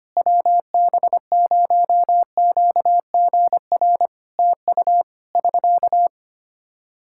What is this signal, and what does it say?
Morse code, heard as a keyed tone, about 25 wpm, 700 Hz. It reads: WB0QGR TU <SK>
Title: Practice QSO - Exchange 4 - Ending - 25wpm